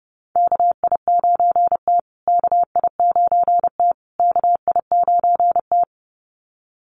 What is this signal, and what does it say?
Morse code, heard as a keyed tone, about 30 wpm, 700 Hz. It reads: KI9T KI9T KI9T
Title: USA Call Signs Repeated 3x 30wpm